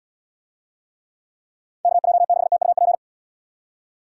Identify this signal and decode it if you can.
Morse code, heard as a keyed tone, about 74 wpm, 700 Hz. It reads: KJ6TUP